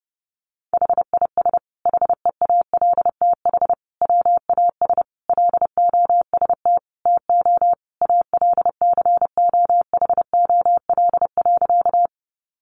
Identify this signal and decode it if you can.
Morse code, heard as a keyed tone, about 30 wpm, 700 Hz. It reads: HIS HEALTH WAS LOST TO ALCOHOL.
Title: Sentences from Top 1400 Words 30wpm